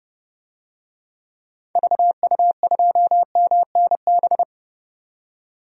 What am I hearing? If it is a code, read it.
Morse code, heard as a keyed tone, about 30 wpm, 700 Hz. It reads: VU2MNB
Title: DXCC-100 Call Signs - 6 Characters - 30wpm